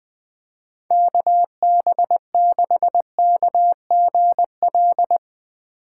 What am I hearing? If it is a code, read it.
Morse code, heard as a keyed tone, about 20 wpm, 700 Hz. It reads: KB6KGL
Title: USA Call Signs - 3x Character Spacing - 20wpm